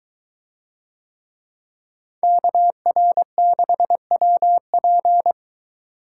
Morse code, heard as a keyed tone, about 22 wpm, 700 Hz. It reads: KR6WP